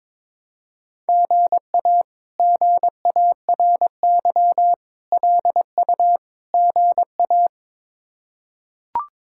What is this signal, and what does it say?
Morse code, heard as a keyed tone, about 22 wpm, 700 Hz. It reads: GA GARY LU GA
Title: K1USN SST QSO Practice - 22wpm